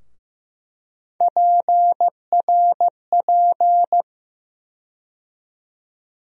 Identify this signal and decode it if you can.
Morse code, heard as a keyed tone, about 15 wpm, 700 Hz. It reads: PRP
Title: USA Call Sign Postfixes - 3 Letters 15wpm